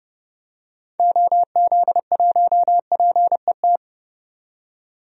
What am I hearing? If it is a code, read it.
Morse code, heard as a keyed tone, about 30 wpm, 700 Hz. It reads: OZ1PET
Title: DXCC-100 Call Signs - 6 Characters - 30wpm